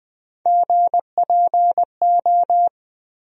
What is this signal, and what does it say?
Morse code, heard as a keyed tone, about 21 wpm, 700 Hz. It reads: GPO
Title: USA Call Sign Postfixes - 3 Letters 20wpm